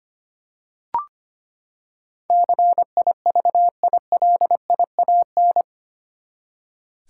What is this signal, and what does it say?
Morse code, heard as a keyed tone, about 25 wpm, 700 Hz. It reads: CIVILIAN
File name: Eight Letter Words - 25wpm